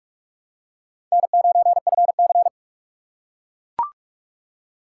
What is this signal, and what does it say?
Morse code, heard as a keyed tone, about 41 wpm, 700 Hz. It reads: N9FC